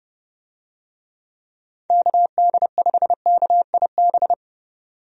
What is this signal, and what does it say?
Morse code, heard as a keyed tone, about 29 wpm, 700 Hz. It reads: KD5KIB
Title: USA Call Signs - 6 Characters - 30wpm